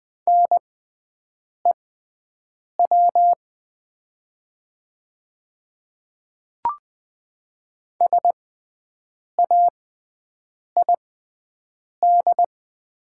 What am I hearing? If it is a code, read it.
Morse code, heard as a keyed tone, about 20 wpm, spaced as for 7 wpm, 700 Hz. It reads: NEW SAID
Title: 6x Character Spacing - Top 200 Words - 20wpm